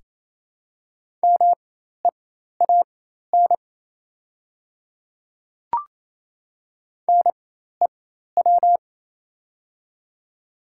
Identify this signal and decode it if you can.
Morse code, heard as a keyed tone, about 28 wpm, spaced as for 13 wpm, 700 Hz. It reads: MEAN NEW
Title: 4x Character Spacing - Top 200 Words - 28wpm